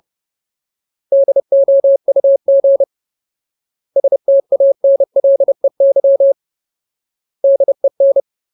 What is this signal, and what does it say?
Morse code, heard as a keyed tone, about 30 wpm, 550 Hz. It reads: DOUG STANLEY DEN
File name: Ultimate Word List / 30WPM Ultimate